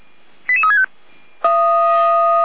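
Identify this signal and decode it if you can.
Five-tone selective call (ZVEI), heard as quick runs of five tones, tones of about 70 ms. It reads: 88365
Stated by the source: ZVEI five-tone call 88352, received on SDR.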